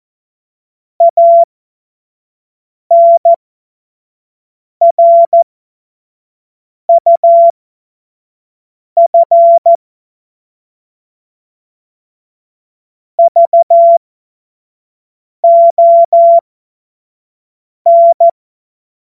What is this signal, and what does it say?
Morse code, heard as a keyed tone, about 14 wpm, spaced as for 5 wpm, 680 Hz. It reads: ANRUF VON
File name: Morsekurs / Witze / Witz 06. April 2021, 25 F70